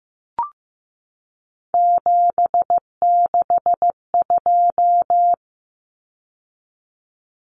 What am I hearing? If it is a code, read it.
Morse code, heard as a keyed tone, about 15 wpm, 700 Hz. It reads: 762
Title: Numbers - 3-Digit 15wpm